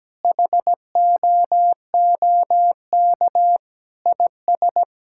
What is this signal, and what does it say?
Morse code, heard as a keyed tone, about 17 wpm, 700 Hz. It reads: HOOK IS
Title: Sentences from Top 1600 Words 17wpm